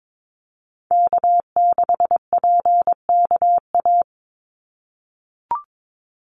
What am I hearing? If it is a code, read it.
Morse code, heard as a keyed tone, about 22 wpm, 700 Hz. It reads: K6PKA
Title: USA Call Signs 22wpm